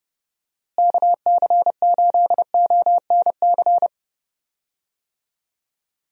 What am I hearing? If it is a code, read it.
Morse code, heard as a keyed tone, about 30 wpm, 700 Hz. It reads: KC8ONC
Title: USA Call Signs - 6 Characters - Repeated 3x - 30wpm